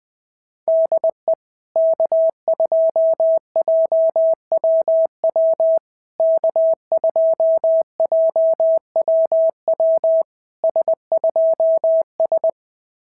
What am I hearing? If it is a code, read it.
Morse code, heard as a keyed tone, about 20 wpm, 650 Hz. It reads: DE K2JWW K2JWW S2S
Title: SOTA QSO Practice 20wpm